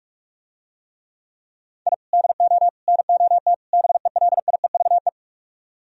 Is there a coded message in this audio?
Yes: Morse code from 1.9 to 5.1 s, heard as a keyed tone, about 47 wpm, 700 Hz. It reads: I DO NOT BELIEVE